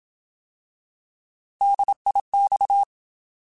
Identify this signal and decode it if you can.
Morse code, heard as a keyed tone, about 27 wpm, 800 Hz. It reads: DIX
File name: Alphabetical / Alphabetical mp3s / Ds @25